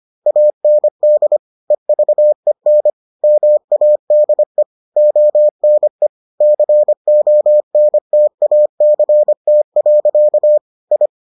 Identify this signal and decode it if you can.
Morse code, heard as a keyed tone, about 25 wpm, 600 Hz. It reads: AND EVEN MADE ONE CONTACT. I